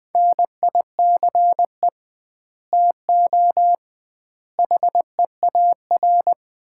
Morse code, heard as a keyed tone, about 20 wpm, 700 Hz. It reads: NICE TO HEAR
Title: Common QSO Phrases - 2x Word-Spacing - 20wpm